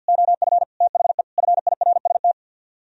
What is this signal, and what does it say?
Morse code, heard as a keyed tone, about 64 wpm, 700 Hz. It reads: OF THE FIRST